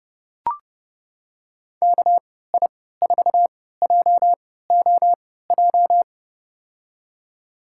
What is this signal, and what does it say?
Morse code, heard as a keyed tone, about 30 wpm, spaced as for 17 wpm, 700 Hz. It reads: KI4JOJ
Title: USA Call Signs - 6 Characters - 3x Character Spacing - 30wpm